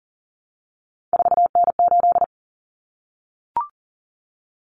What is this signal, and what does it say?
Morse code, heard as a keyed tone, about 40 wpm, 700 Hz. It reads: 4N8
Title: Common Cut Numbers - 3-Digit 40wpm